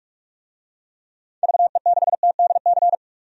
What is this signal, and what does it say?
Morse code, heard as a keyed tone, about 44 wpm, 700 Hz. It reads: VE6TDC
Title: DXCC-100 Call Signs - 6 Characters - 45wpm